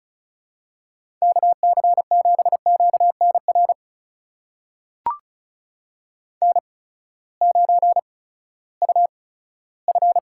Morse code, heard as a keyed tone, about 36 wpm, spaced as for 18 wpm, 700 Hz. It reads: KC7QNR N9UF
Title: USA Call Signs - 8x Character Spacing - 35wpm